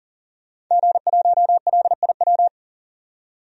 Morse code, heard as a keyed tone, about 37 wpm, 700 Hz. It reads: G1LIW
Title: DXCC-100 Call Signs - 5 Characters - 40wpm